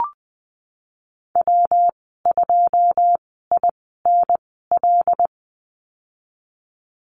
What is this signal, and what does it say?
Morse code, heard as a keyed tone, about 20 wpm, spaced as for 15 wpm, 700 Hz. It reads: W2INL
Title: 161 Lesson 26g - Letter g - Call Signs 20wpm